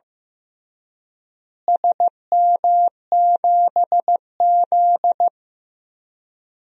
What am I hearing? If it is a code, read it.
Morse code, heard as a keyed tone, about 15 wpm, 700 Hz. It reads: SM7Z
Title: DXCC-100 Call Signs - 4 Characters - Repeated 3x - 15wpm